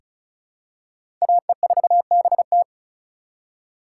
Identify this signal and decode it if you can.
Morse code, heard as a keyed tone, about 34 wpm, 700 Hz. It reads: AE4BT
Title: USA Call Signs - 5 Characters - 4x Character Spacing - 35wpm